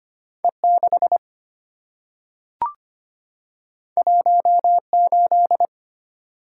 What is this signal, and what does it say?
Morse code, heard as a keyed tone, about 25 wpm, 700 Hz. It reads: E6 18
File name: Two Letter-Number Combinations - 25wpm